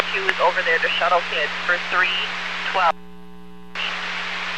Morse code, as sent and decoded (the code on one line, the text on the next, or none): none